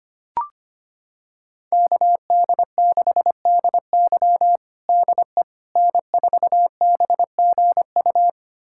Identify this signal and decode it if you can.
Morse code, heard as a keyed tone, about 25 wpm, 700 Hz. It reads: KD6DY DE N4BGU
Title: USA Callsign DE Callsign - 5 Characters - 25wpm